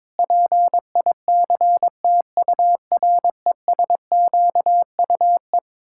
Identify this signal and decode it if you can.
Morse code, heard as a keyed tone, about 22 wpm, 700 Hz. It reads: PICTURESQUE
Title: Pangram Challenge v2 - 22wpm